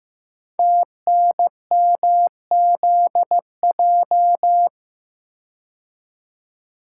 Morse code, heard as a keyed tone, about 15 wpm, 690 Hz. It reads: TNMZJ
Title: Code Groups - 5 Letters - 15wpm